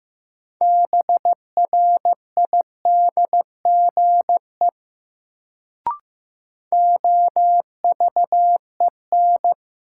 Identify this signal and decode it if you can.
Morse code, heard as a keyed tone, about 15 wpm, 700 Hz. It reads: BRIDGE OVEN
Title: Sentences from Top 1200 Words - New Word Drill Alt - 15wpm